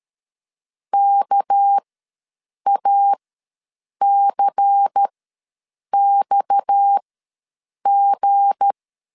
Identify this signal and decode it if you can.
Morse code, heard as a keyed tone, about 13 wpm, spaced as for 7 wpm, 800 Hz. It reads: KACXG
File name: LEZIONI / 7 GF9Z / 7 GF ws 12wpm 3140